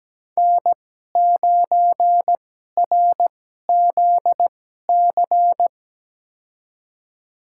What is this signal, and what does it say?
Morse code, heard as a keyed tone, about 17 wpm, spaced as for 12 wpm, 700 Hz. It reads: N9RZC
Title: USA Call Signs - 5 Characters - 2x Character Spacing - 17wpm